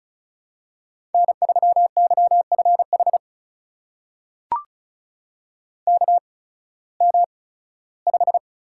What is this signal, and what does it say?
Morse code, heard as a keyed tone, about 36 wpm, spaced as for 19 wpm, 700 Hz. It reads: N3YFH KM5
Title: USA Call Signs - 5 Characters - 8x Character Spacing - 35wpm